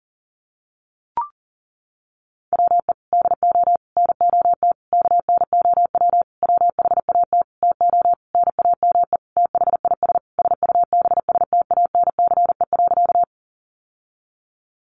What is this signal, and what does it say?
Morse code, heard as a keyed tone, about 37 wpm, 700 Hz. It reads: WE DO NOT KNOW WHAT TO NAME THIS SUBSTANCE.
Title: Sentences from Top 1500 Words 40wpm